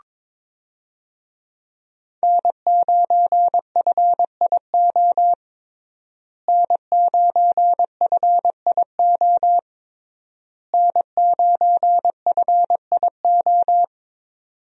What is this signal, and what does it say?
Morse code, heard as a keyed tone, about 22 wpm, 700 Hz. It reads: N9FIO N9FIO N9FIO
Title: USA Call Signs - 5 Characters - Repeated 3x - 22wpm